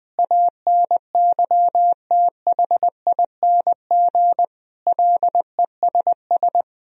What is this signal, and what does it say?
Morse code, heard as a keyed tone, about 21 wpm, 700 Hz. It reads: ANYTHING LESS
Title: Sentences from Top 1000 Words - Review Part 2 - 20wpm